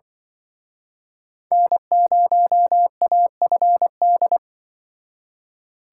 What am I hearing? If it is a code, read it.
Morse code, heard as a keyed tone, about 26 wpm, 700 Hz. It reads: N0AFD